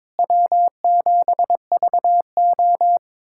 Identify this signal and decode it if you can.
Morse code, heard as a keyed tone, about 22 wpm, 700 Hz. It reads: W7VO
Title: USA Callsign DE Callsign - 4 Characters - 22wpm